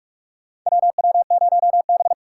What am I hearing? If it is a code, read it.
Morse code, heard as a keyed tone, about 44 wpm, 700 Hz. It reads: WW0B
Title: USA Call Signs - 6x Character Spacing - 45wpm